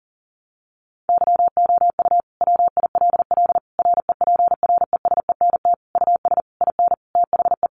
Morse code, heard as a keyed tone, about 40 wpm, 700 Hz. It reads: YOU WILL REPRESENT US IN THE
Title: Sentences from Top 1200 Words 40wpm